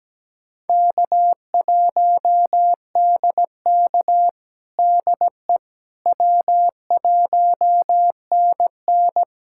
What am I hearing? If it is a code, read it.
Morse code, heard as a keyed tone, about 17 wpm, 700 Hz. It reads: K1DK DE W1NN
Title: USA Callsign DE Callsign - 4 Characters - 17wpm